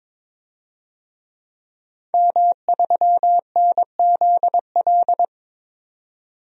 Morse code, heard as a keyed tone, about 22 wpm, 700 Hz. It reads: M3NZL
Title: DXCC-100 Call Signs - 5 Characters - 22wpm